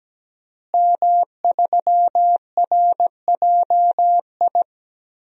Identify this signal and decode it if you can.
Morse code, heard as a keyed tone, about 17 wpm, 700 Hz. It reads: M3RJI
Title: DXCC-100 Call Signs - 5 Characters - Repeated 2x - 17wpm